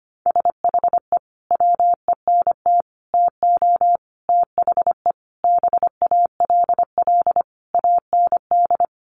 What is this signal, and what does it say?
Morse code, heard as a keyed tone, about 25 wpm, 700 Hz. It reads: SHE WENT TO THE BALL AND
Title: Sentences from Top 900 Words 25wpm